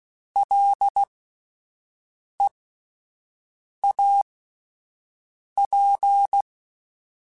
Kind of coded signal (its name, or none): Morse code